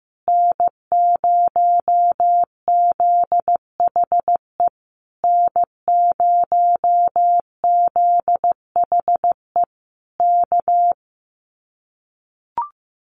Morse code, heard as a keyed tone, about 15 wpm, 700 Hz. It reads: N0ZHE N0ZHE K
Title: Common QSO Phrases Encore 15wpm